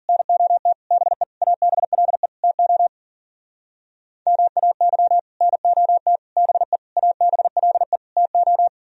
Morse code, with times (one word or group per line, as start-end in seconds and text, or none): 0.1-0.7 NOT
0.9-1.2 BE
1.4-2.3 ABLE
2.4-2.9 TO
4.3-5.2 MAY
5.4-6.2 NOT
6.4-6.8 BE
7.0-8.0 ABLE
8.2-8.7 TO